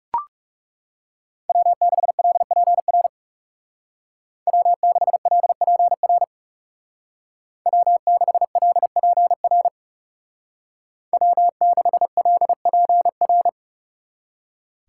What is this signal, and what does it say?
Morse code, heard as a keyed tone, about 35 wpm, 700 Hz. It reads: W6LPR W6LPR W6LPR W6LPR
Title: Speed Racing - USA 1x3 Call Signs 30wpm